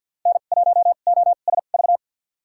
Morse code, heard as a keyed tone, about 52 wpm, spaced as for 37 wpm, 690 Hz. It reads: N1OSV
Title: USA Call Signs - 5 Characters - 2x Character Spacing - 50wpm